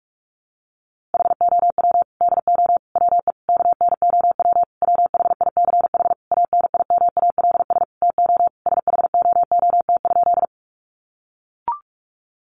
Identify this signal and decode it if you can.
Morse code, heard as a keyed tone, about 44 wpm, 700 Hz. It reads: HOW DO WE KNOW WHICH ANIMALS TO SHOOT?
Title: A Sound Of Thunder 45wpm